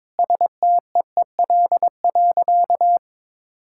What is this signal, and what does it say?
Morse code, heard as a keyed tone, about 21 wpm, 690 Hz. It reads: STEEL.
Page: morse-code-ninja-practice-22wpm